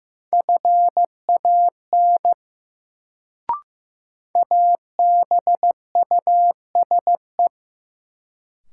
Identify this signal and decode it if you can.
Morse code, heard as a keyed tone, about 15 wpm, 700 Hz. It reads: FAN ABUSE
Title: Sentences from Top 1300 Words - New Word Drill Alt - 15wpm